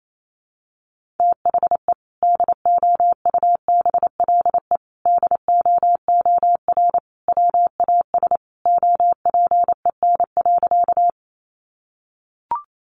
Morse code, heard as a keyed tone, about 28 wpm, 700 Hz. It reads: THE DOUBLE DOOR WAS OPEN.
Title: Sentences from Top 1400 Words 28wpm